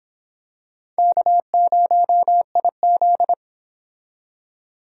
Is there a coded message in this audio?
Yes: Morse code from 1.0 to 3.3 s, heard as a keyed tone, about 26 wpm, 700 Hz. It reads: K0IZ